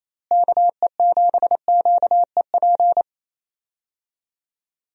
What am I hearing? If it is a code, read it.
Morse code, heard as a keyed tone, about 27 wpm, 700 Hz. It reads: KE7QEP
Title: USA Call Signs - 6 Characters - Repeated 3x - 28wpm